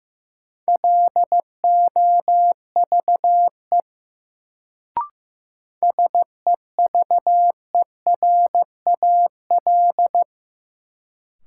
Morse code, heard as a keyed tone, about 15 wpm, 700 Hz. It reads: LOVE SEVERAL
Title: Sentences from Top 400 Words - New Word Drill - 15wpm